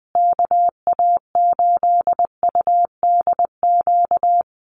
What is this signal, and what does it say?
Morse code, heard as a keyed tone, about 19 wpm, 700 Hz. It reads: KA8UDQ
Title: USA Call Signs Repeated 3x 20wpm